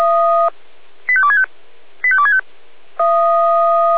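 Five-tone selective call (ZVEI), heard as quick runs of five tones, tones of about 70 ms. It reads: 85265 85265
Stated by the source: ZVEI five-tone selective call 85243, received via SDR.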